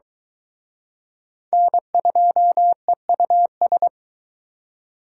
Morse code, heard as a keyed tone, about 23 wpm, 700 Hz. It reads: N2EUS